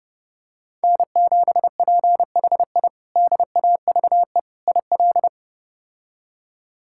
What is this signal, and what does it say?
Morse code, heard as a keyed tone, about 32 wpm, 700 Hz. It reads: N7PHI DAVE IL
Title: K1USN SST QSO Practice - 30wpm